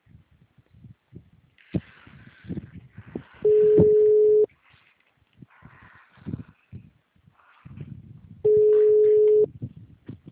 Ringback tone (Europe, 425 Hz).